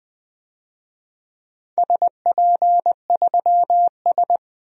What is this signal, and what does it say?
Morse code, heard as a keyed tone, about 20 wpm, 700 Hz. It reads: SP3S
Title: DXCC-100 Call Signs - 4 Characters - 20wpm